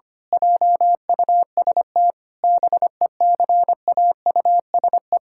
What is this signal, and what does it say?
Morse code, heard as a keyed tone, about 25 wpm, 700 Hz. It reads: JUST BECAUSE
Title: Pangram Challenge v2 - 25wpm